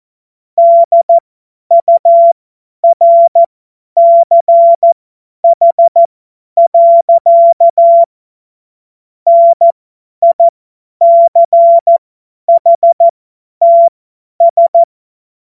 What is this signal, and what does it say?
Morse code, heard as a keyed tone, about 14 wpm, spaced as for 10 wpm, 680 Hz. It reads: DURCH. NICHTS
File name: Morsekurs / Witze / Witz 23. August 2022 50 F70